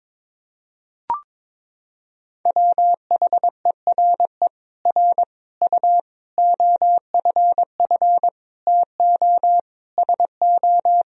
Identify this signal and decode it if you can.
Morse code, heard as a keyed tone, about 22 wpm, 700 Hz. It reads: WHERE R U OFF TO SO
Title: Sentences from Top 400 Words and Abbreviations 22wpm